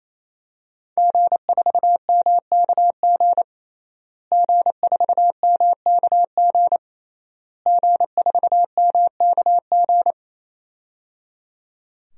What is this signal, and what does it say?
Morse code, heard as a keyed tone, about 28 wpm, 700 Hz. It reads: G4MKG G4MKG G4MKG
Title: DXCC-100 Call Signs - 5 Characters - Repeated 3x - 28wpm